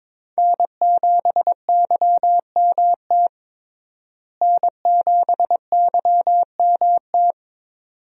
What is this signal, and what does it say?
Morse code, heard as a keyed tone, about 22 wpm, 700 Hz. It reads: N7YMT N7YMT
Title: USA Call Signs - 5 Characters - Repeated 2x - 22wpm